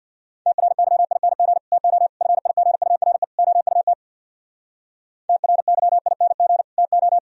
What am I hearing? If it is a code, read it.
Morse code, heard as a keyed tone, about 64 wpm, 700 Hz. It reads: TRYING TO FIGURE OUT TRYING TO